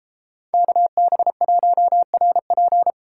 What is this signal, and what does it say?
Morse code, heard as a keyed tone, about 34 wpm, 700 Hz. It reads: KB1RP